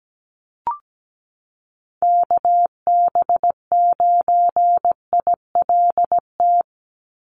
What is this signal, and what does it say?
Morse code, heard as a keyed tone, about 17 wpm, 700 Hz. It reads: KB9ILT
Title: NA SOTA Activator Callsigns 17wpm